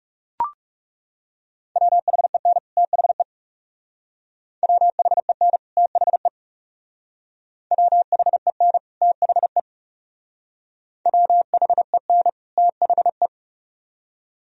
Morse code, heard as a keyed tone, about 39 wpm, 700 Hz. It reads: WHEN THE WHEN THE WHEN THE WHEN THE